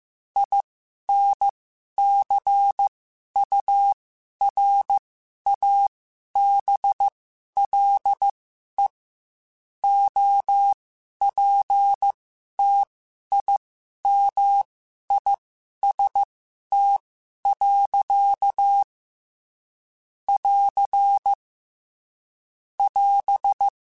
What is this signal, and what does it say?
Morse code, heard as a keyed tone, about 15 wpm, spaced as for 11 wpm, 800 Hz. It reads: INCURABLE OPTIMIST. <AR> <AS>
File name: Quotes / Quotes-mp3s / Optimist @13